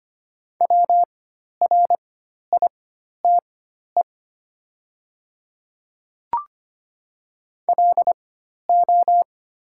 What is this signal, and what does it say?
Morse code, heard as a keyed tone, about 25 wpm, spaced as for 12 wpm, 700 Hz. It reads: WRITE LO
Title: 4x Character Spacing - Top 100 Words - 25wpm